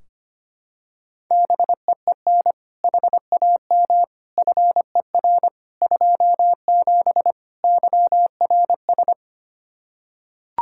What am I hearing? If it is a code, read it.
Morse code, heard as a keyed tone, about 25 wpm, 700 Hz. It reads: BEEN HAM FER 27 YRS
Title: K1USN SST QSO Practice 25wpm